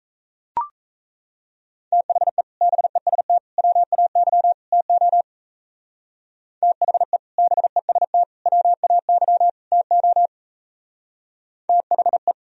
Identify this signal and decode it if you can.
Morse code, heard as a keyed tone, about 34 wpm, 700 Hz. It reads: THE BEST WAY TO THE BEST WAY TO THE
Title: Speed-Racing - Sets of 4 Words 28wpm